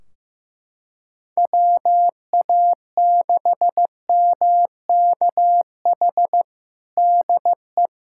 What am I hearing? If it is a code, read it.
Morse code, heard as a keyed tone, about 14 wpm, 700 Hz. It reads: WA6MKH DE
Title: Practice QSO - Exchange 4 - Ending - 15wpm